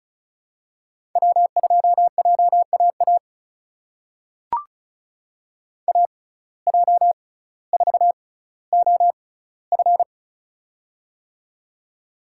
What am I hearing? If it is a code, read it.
Morse code, heard as a keyed tone, about 36 wpm, spaced as for 18 wpm, 700 Hz. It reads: W2JAA AJ4OF
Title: USA Call Signs - 5 Characters - 6x Character Spacing - 35wpm